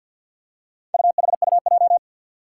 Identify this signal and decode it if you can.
Morse code, heard as a keyed tone, about 47 wpm, 700 Hz. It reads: UHUJ
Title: Four Challenging Letter-Number Combinations 50wpm